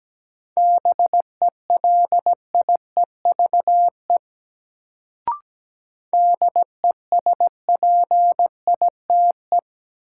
Morse code, heard as a keyed tone, about 17 wpm, 700 Hz. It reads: BELIEVE DESPITE
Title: Sets of 4 Words - New Word Drill - 17wpm